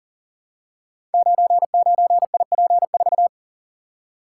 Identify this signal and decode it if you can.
Morse code, heard as a keyed tone, about 40 wpm, 700 Hz. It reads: 99IP4
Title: Code Groups - 5 Letter:Numbers - 40wpm